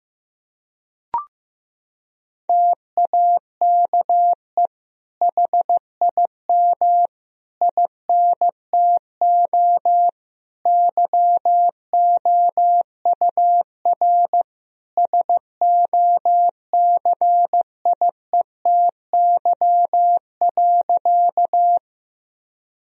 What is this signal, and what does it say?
Morse code, heard as a keyed tone, about 15 wpm, 700 Hz. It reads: TAKE HIM INTO YOUR SOCIETY.
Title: Sentences from Top 1000 Words - Review Part 4 - 15wpm